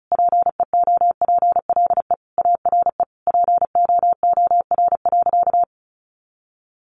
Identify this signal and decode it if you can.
Morse code, heard as a keyed tone, about 34 wpm, 700 Hz. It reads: PEOPLE ARE POOR.
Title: Sentences from Top 1100 Words 35wpm